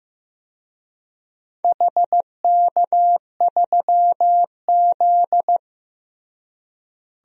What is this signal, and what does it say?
Morse code, heard as a keyed tone, about 15 wpm, 700 Hz. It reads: HK3Z
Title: DXCC-100 Call Signs - 4 Characters - Repeated 2x - 15wpm